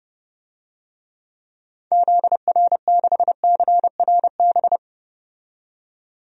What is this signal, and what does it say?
Morse code, heard as a keyed tone, about 30 wpm, 700 Hz. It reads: ZR6CRB
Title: DXCC-100 Call Signs - 6 Characters - 30wpm